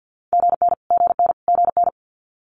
Morse code, heard as a keyed tone, about 50 wpm, 690 Hz. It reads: GN GN GN
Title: Multi-letter Phonemes Repeated 3x 50wpm